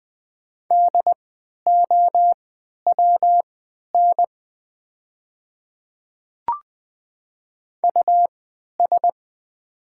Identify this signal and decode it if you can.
Morse code, heard as a keyed tone, about 20 wpm, spaced as for 11 wpm, 700 Hz. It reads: DOWN US
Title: 3x Character Spacing - Top 300 Words - 20wpm